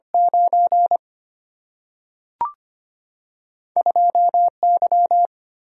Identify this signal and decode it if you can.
Morse code, heard as a keyed tone, about 25 wpm, 700 Hz. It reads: 9 2 Y
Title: Single-Letter Single-Number 25wpm